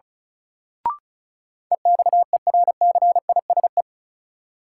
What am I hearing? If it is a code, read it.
Morse code, heard as a keyed tone, about 36 wpm, 700 Hz. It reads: EXERCISE
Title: Sentences from Top 1200 Words - New Word Drill Alt - 35wpm